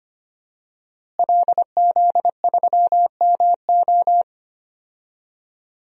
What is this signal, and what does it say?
Morse code, heard as a keyed tone, about 25 wpm, 700 Hz. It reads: LZ3MO
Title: DXCC-100 Call Signs - 5 Characters - Repeated 2x - 25wpm